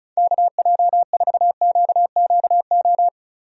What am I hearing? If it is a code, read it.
Morse code, heard as a keyed tone, about 36 wpm, 700 Hz. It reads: KJ4QQO
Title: USA Call Signs - 6 Characters - 35wpm